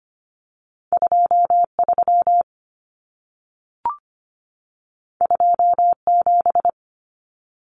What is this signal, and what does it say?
Morse code, heard as a keyed tone, about 25 wpm, 700 Hz. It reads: 23 27
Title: Common Cut Numbers - 2-Digit 25wpm